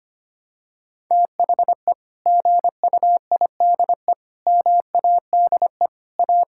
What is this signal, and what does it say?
Morse code, heard as a keyed tone, about 25 wpm, 700 Hz. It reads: THE GUIDE MADE A
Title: Sentences from Top 1000 Words - Review Part 6 - 25wpm